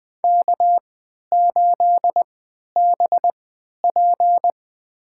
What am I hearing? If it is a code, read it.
Morse code, heard as a keyed tone, about 20 wpm, spaced as for 11 wpm, 700 Hz. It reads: K8BP
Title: USA Call Signs - 3x Character Spacing - 20wpm